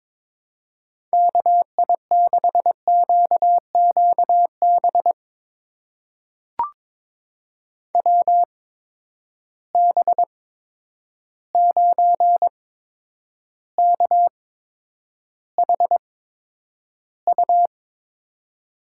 Morse code, heard as a keyed tone, about 22 wpm, spaced as for 9 wpm, 700 Hz. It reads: KI6QQB WB9KHU
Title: USA Call Signs - 6 Characters - 8x Character Spacing - 22wpm